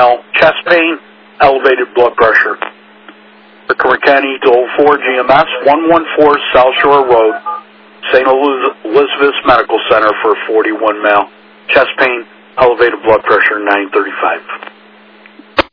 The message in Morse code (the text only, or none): none